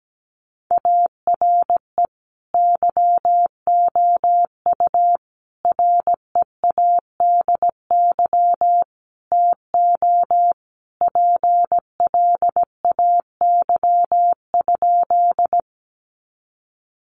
Morse code, heard as a keyed tone, about 17 wpm, 700 Hz. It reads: ARE YOU READY TO PLAY?
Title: Sentences from Top 400 Words 17wpm